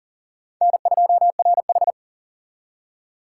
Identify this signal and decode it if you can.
Morse code, heard as a keyed tone, about 51 wpm, 700 Hz. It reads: N2RH